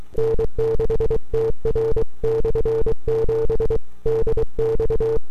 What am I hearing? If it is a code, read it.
Morse code, heard as a keyed tone, about 23 wpm, 450 Hz. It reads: N6TR/7 DX
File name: n6tr / 12Oct2009-160